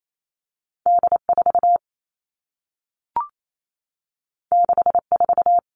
Morse code, heard as a keyed tone, about 28 wpm, 700 Hz. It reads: D 4 6 4
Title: Two Challenging Letter-Number Combinations 28wpm